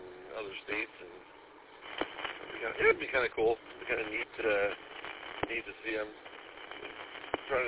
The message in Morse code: none

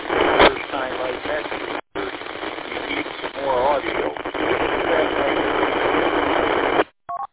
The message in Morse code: none

